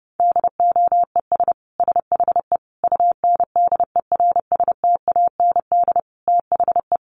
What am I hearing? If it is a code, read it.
Morse code, heard as a keyed tone, about 30 wpm, 700 Hz. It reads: DOES SHE UNDERSTAND THE
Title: Sentences from Top 1100 Words 30wpm